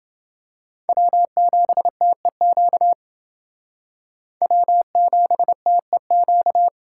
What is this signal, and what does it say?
Morse code, heard as a keyed tone, about 28 wpm, 700 Hz. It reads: W7TEQ W7TEQ